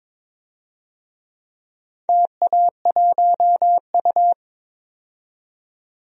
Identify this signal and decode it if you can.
Morse code, heard as a keyed tone, about 22 wpm, 700 Hz. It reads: TA1U
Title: DXCC-100 Call Signs - 4 Characters - 22wpm